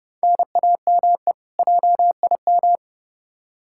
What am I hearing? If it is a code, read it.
Morse code, heard as a keyed tone, about 31 wpm, 700 Hz. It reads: NAME JIM
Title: K1USN SST QSO Practice 30wpm